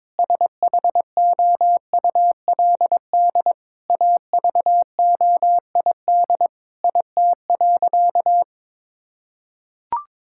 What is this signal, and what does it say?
Morse code, heard as a keyed tone, about 22 wpm, 700 Hz. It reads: SHOULD AVOID IT.
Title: Sentences from Top 1000 Words - Review Part 6 - 22wpm